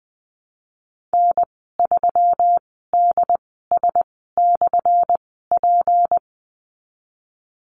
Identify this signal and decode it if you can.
Morse code, heard as a keyed tone, about 20 wpm, spaced as for 14 wpm, 700 Hz. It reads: N3DS/P
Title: 179 Lesson 29g - Slash - Call Signs 20wpm